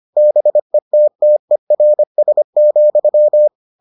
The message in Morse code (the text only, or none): BETTERS,